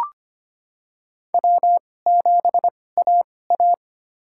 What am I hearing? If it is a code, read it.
Morse code, heard as a keyed tone, about 25 wpm, spaced as for 18 wpm, 700 Hz. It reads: W7AA
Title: USA Call Signs - 4 Characters - 2x Character Spacing - 25wpm